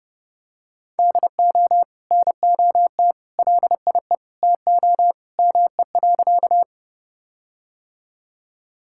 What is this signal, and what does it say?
Morse code, heard as a keyed tone, about 28 wpm, 700 Hz. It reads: DO NOT LIE TO ME.
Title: Sentences from Top 1600 Words 30wpm